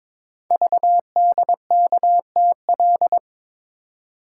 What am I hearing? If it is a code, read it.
Morse code, heard as a keyed tone, about 22 wpm, 700 Hz. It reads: VDKTL
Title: Code Groups - 5 Letters - 22wpm